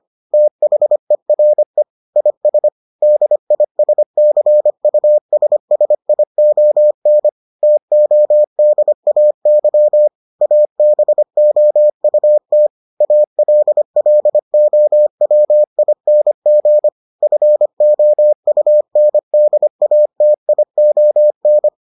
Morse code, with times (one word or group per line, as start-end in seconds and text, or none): 0.3-1.8 THERE
2.2-2.7 IS
3.0-7.3 DISCUSSION
7.6-10.1 TODAY
10.4-12.7 ABOUT
13.0-16.9 ALLOWING
17.2-21.7 FOUNDATION